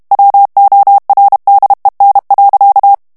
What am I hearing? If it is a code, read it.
Morse code, heard as a keyed tone, about 32 wpm, 800 Hz. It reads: WORDEN.
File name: morse mp3 / HOLLANDAIS texte clair 1 vitesse 30 mots minute